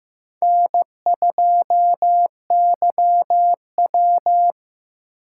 Morse code, heard as a keyed tone, about 15 wpm, 700 Hz. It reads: N2YW
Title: USA Call Signs - 6x Character Spacing - 15wpm